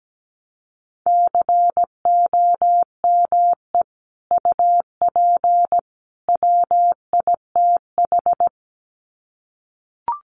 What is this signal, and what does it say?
Morse code, heard as a keyed tone, about 17 wpm, 700 Hz. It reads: COME UP WITH
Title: Sets of 3 Words 17wpm